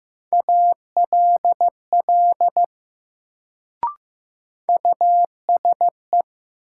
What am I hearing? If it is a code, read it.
Morse code, heard as a keyed tone, about 15 wpm, 700 Hz. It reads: ALL USE
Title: Sentences from Top 100 Words - New Word Drill - 15wpm